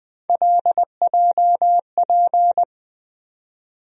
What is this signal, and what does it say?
Morse code, heard as a keyed tone, about 20 wpm, 700 Hz. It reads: LJP
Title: USA Call Sign Postfixes - 3 Letters 20wpm